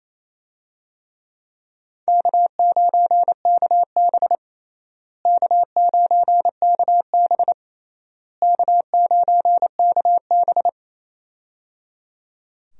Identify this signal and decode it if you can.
Morse code, heard as a keyed tone, about 28 wpm, 700 Hz. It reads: K9KB K9KB K9KB
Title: USA Call Signs - 4 Characters - Repeated 3x - 28wpm